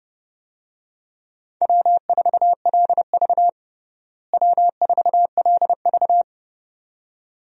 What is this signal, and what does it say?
Morse code, heard as a keyed tone, about 30 wpm, 700 Hz. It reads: W4LV W4LV
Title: USA Call Signs - 4 Characters - Repeated 2x - 30wpm